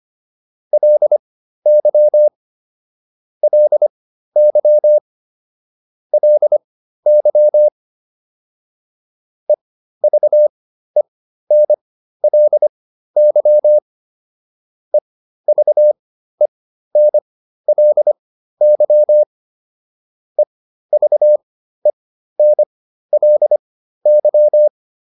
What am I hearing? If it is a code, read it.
Morse code, heard as a keyed tone, about 25 wpm, spaced as for 13 wpm, 600 Hz. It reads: LY LY LY EVENLY EVENLY EVENLY